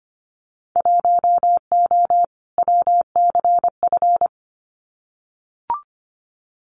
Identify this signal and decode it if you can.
Morse code, heard as a keyed tone, about 25 wpm, 690 Hz. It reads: 1O WCF
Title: Winter Field Day Exchanges 25wpm